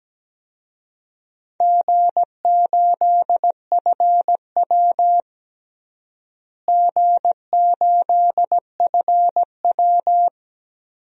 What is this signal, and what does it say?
Morse code, heard as a keyed tone, about 17 wpm, 700 Hz. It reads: G8FW G8FW
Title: DXCC-100 Call Signs - 4 Characters - Repeated 3x - 17wpm